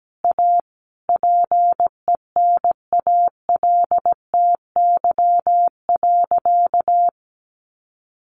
Morse code, heard as a keyed tone, about 17 wpm, 700 Hz. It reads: A PENALTY.
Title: A Sound Of Thunder 17wpm